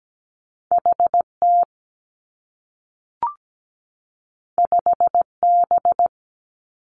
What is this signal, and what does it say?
Morse code, heard as a keyed tone, about 17 wpm, 700 Hz. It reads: H T 5 B
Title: Two Challenging Letter-Number Combinations 17wpm